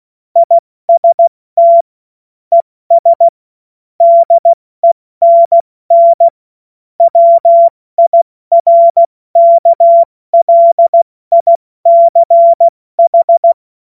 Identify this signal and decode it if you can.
Morse code, heard as a keyed tone, about 16 wpm, 680 Hz. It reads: IST ES DENN WIRKLICH
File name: Morsekurs / Witze / Witz 26 März 2024 70 F80